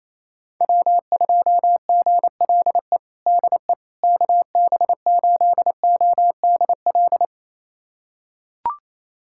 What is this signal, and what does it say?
Morse code, heard as a keyed tone, about 28 wpm, 700 Hz. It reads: W2GLE DE KB8ODL
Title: USA Callsign DE Callsign 28wpm